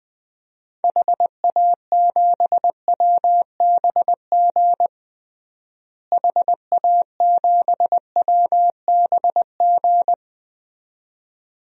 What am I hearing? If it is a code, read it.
Morse code, heard as a keyed tone, about 20 wpm, 700 Hz. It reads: HA7WBG HA7WBG
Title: DXCC-100 Call Signs - 6 Characters - Repeated 2x - 20wpm